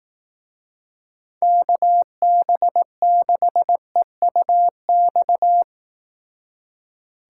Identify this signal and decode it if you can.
Morse code, heard as a keyed tone, about 18 wpm, 700 Hz. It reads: KB6EUX
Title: Speed Racing - USA 2x3 Call Signs 15wpm